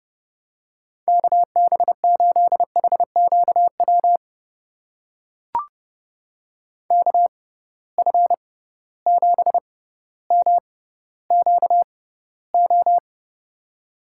Morse code, heard as a keyed tone, about 30 wpm, spaced as for 15 wpm, 700 Hz. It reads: KB8HQW KF7MQO
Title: USA Call Signs - 6 Characters - 6x Character Spacing - 30wpm